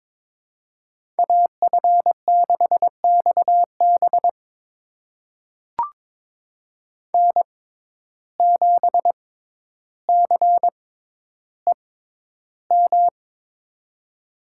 Morse code, heard as a keyed tone, about 22 wpm, spaced as for 11 wpm, 700 Hz. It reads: AF6XB N7CEM
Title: USA Call Signs - 6x Character Spacing - 22wpm